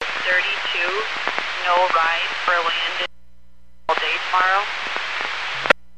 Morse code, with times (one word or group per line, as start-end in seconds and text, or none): none